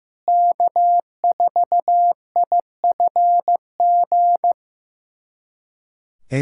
Morse code, heard as a keyed tone, about 15 wpm, 700 Hz. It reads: K4IFG
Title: USA Callsign DE Callsign - 5 Characters - 15wpm